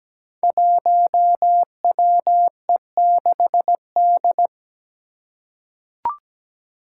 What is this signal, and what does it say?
Morse code, heard as a keyed tone, about 17 wpm, 700 Hz. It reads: 1WE6D
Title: Five Challenging Letter-Number Combinations 17wpm